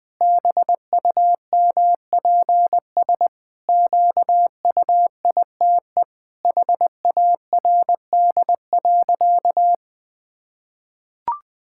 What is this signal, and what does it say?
Morse code, heard as a keyed tone, about 19 wpm, 700 Hz. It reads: BUMPS QUITE HARD.
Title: Pangram Challenge v2 - 20wpm